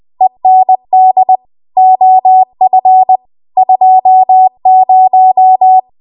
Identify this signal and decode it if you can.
Morse code, heard as a keyed tone, about 20 wpm, 750 Hz. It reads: END OF 20
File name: Archive / 20 WPM / 20WPM